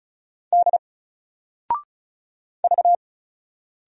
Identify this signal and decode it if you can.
Morse code, heard as a keyed tone, about 36 wpm, 700 Hz. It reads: D V
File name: Single-Letter-Number-35wpm